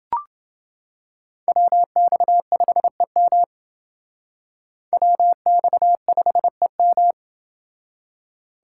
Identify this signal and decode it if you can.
Morse code, heard as a keyed tone, about 28 wpm, 700 Hz. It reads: WX5EM WX5EM